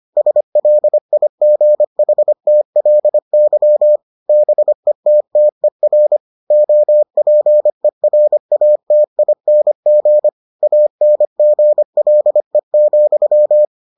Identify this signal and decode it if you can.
Morse code, heard as a keyed tone, about 25 wpm, 600 Hz. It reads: SLIGHTLY BETTER OPERATING ANGLE,